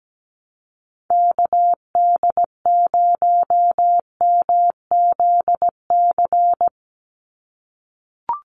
Morse code, heard as a keyed tone, about 17 wpm, 700 Hz. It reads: KD0MZC
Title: USA Call Signs 17wpm